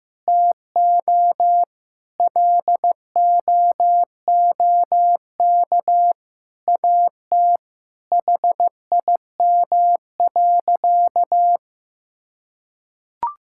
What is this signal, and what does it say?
Morse code, heard as a keyed tone, about 15 wpm, 700 Hz. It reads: TO LOOK AT HIM.
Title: Sentences from Top 1600 Words 15wpm